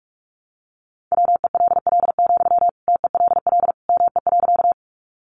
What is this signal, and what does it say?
Morse code, heard as a keyed tone, about 44 wpm, 700 Hz. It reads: WELL, TELL ME.
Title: Sentences from Top 200 Words 45wpm